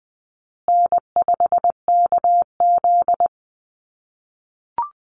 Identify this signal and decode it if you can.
Morse code, heard as a keyed tone, about 20 wpm, 700 Hz. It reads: N5KZ
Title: USA Call Signs Repeated 3x 20wpm